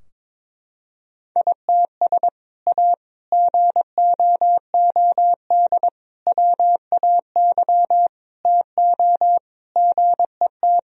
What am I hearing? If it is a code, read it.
Morse code, heard as a keyed tone, about 21 wpm, 700 Hz. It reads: ITS A GOOD WAY TO GET
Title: Sentences from Top 1000 Words - Review Part 3 - 22wpm